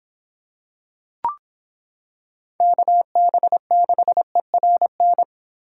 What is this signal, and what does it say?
Morse code, heard as a keyed tone, about 26 wpm, 700 Hz. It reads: KB6ERN